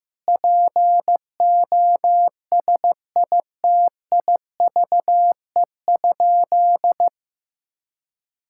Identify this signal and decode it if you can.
Morse code, heard as a keyed tone, about 15 wpm, 700 Hz. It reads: POSITIVE?
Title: Sentences from Top 1000 Words - Review Part 5 - 15wpm